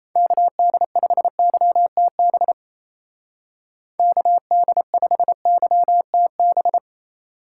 Morse code, heard as a keyed tone, about 33 wpm, 700 Hz. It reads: KD5YTB KD5YTB
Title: Speed Racing - USA 2x3 Call Signs 28wpm